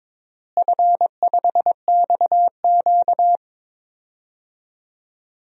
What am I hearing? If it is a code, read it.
Morse code, heard as a keyed tone, about 22 wpm, 700 Hz. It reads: F5XQ
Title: DXCC-100 Call Signs - 4 Characters - Repeated 3x - 22wpm